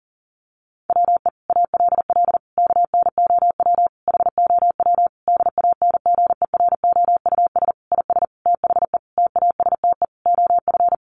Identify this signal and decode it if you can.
Morse code, heard as a keyed tone, about 40 wpm, 700 Hz. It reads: WE ALL KNOW HOW DANGEROUS IS THE TASTE OF
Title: Sentences from Top 1200 Words 40wpm